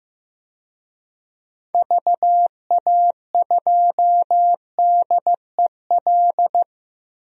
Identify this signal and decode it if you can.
Morse code, heard as a keyed tone, about 16 wpm, 700 Hz. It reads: VA2DEL
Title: DXCC-100 Call Signs - 6 Characters - 15wpm